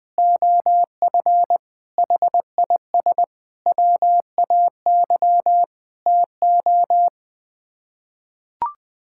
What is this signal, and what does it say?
Morse code, heard as a keyed tone, about 20 wpm, 700 Hz. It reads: OF HIS WAY TO
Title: Sets of 6 Words - 20wpm